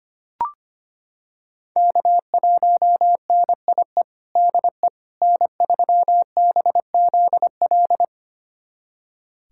Morse code, heard as a keyed tone, about 25 wpm, 700 Hz. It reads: K1NIE DE N3BZL
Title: USA Callsign DE Callsign - 5 Characters - 25wpm